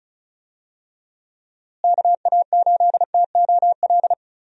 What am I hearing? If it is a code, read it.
Morse code, heard as a keyed tone, about 36 wpm, 700 Hz. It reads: KA8TOL
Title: USA Call Signs - 6 Characters - Repeated 3x - 35wpm